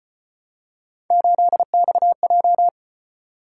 Morse code, heard as a keyed tone, about 32 wpm, 700 Hz. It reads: 8XJ